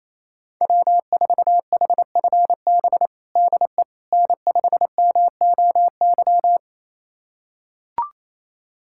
Morse code, heard as a keyed tone, about 27 wpm, 700 Hz. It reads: W4HFB DE N5MOY
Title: USA Callsign DE Callsign - 5 Characters - 28wpm